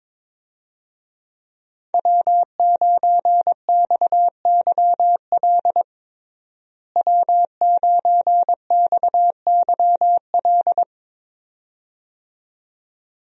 Morse code, heard as a keyed tone, about 22 wpm, 700 Hz. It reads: W9XYL W9XYL
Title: USA Call Signs - 5 Characters - Repeated 2x - 22wpm